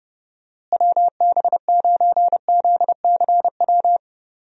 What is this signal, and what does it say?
Morse code, heard as a keyed tone, about 30 wpm, 700 Hz. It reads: WB9ZCW
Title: USA Call Signs - 6 Characters - Repeated 2x - 30wpm